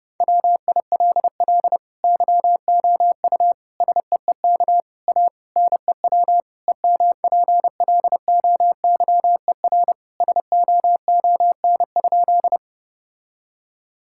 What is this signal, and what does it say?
Morse code, heard as a keyed tone, about 30 wpm, 700 Hz. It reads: WILL YOU SEEK A NEW EMPLOYER SOON?
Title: Sentences from Top 1600 Words 30wpm